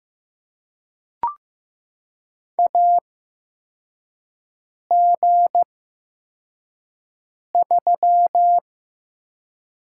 Morse code, heard as a keyed tone, about 15 wpm, spaced as for 4 wpm, 700 Hz. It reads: AG3
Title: USA Call Signs - 4 Characters - 8x Character Spacing - 15wpm